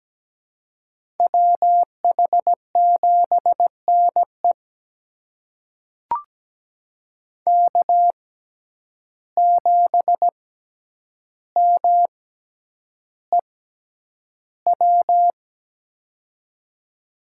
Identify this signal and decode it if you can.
Morse code, heard as a keyed tone, about 17 wpm, spaced as for 9 wpm, 700 Hz. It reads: WH7NE K7MEW
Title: USA Call Signs - 5 Characters - 6x Character Spacing - 17wpm